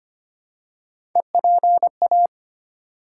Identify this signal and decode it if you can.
Morse code, heard as a keyed tone, about 25 wpm, 700 Hz. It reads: EPA
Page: morse-code-ninja-practice-25wpm